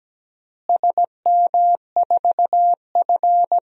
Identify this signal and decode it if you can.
Morse code, heard as a keyed tone, about 17 wpm, 700 Hz. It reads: SM4F
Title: DXCC-100 Call Signs - 4 Characters - Repeated 3x - 17wpm